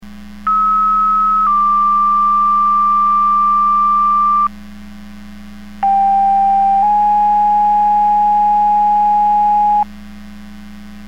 Two-tone page, heard as a one-second tone, then a three-second tone, about 1290 then 1180 Hz, about 790 then 830 Hz. It is heard from 0.5 to 9.8 s.